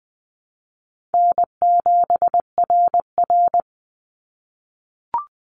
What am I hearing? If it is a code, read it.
Morse code, heard as a keyed tone, about 20 wpm, 700 Hz. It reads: N7RR
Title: Easy Call Signs - 20wpm